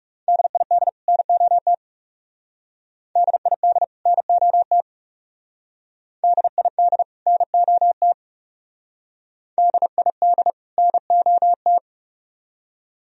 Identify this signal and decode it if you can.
Morse code, heard as a keyed tone, about 41 wpm, 700 Hz. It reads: DID NOT DID NOT DID NOT DID NOT